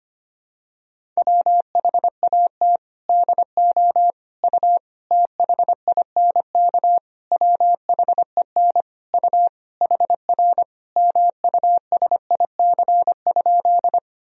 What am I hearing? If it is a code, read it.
Morse code, heard as a keyed tone, about 25 wpm, 700 Hz. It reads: WHAT DO U THINK WHEN U HR MUSIC?
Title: Sentences from Top 300 Words and Abbreviations 25wpm